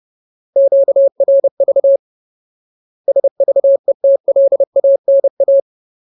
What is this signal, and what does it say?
Morse code, heard as a keyed tone, about 30 wpm, 550 Hz. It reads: QRV SVETLANA
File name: Ultimate Word List / 30WPM Ultimate